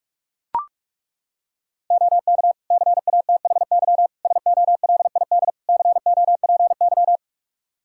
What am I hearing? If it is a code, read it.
Morse code, heard as a keyed tone, about 42 wpm, 700 Hz. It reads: OK CATHY SOLID COPY
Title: K1USN SST QSO Practice 45wpm